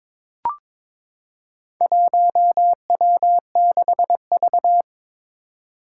Morse code, heard as a keyed tone, about 22 wpm, 700 Hz. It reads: 1W6V
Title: Four Challenging Letter-Number Combinations 22wpm